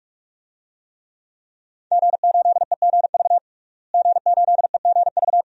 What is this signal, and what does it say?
Morse code, heard as a keyed tone, about 46 wpm, 700 Hz. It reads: G8EGV G8EGV
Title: DXCC-100 Call Signs - 5 Characters - Repeated 2x - 45wpm